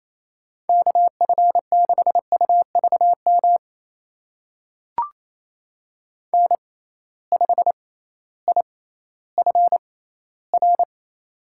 Morse code, heard as a keyed tone, about 28 wpm, spaced as for 15 wpm, 700 Hz. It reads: KF6UVM N5IFR
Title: USA Call Signs - 6x Character Spacing - 28wpm